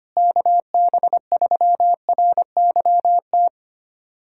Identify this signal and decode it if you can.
Morse code, heard as a keyed tone, about 25 wpm, 700 Hz. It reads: KB3RYT
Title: USA Callsign DE Callsign - 6 Characters - 25wpm